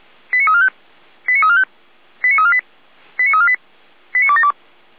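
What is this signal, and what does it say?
Five-tone selective call (ZVEI), heard as quick runs of five tones, tones of about 70 ms. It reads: 89345 89345 89348 89348 89272
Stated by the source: Received on an SDR as ZVEI five-tone call, address 89320.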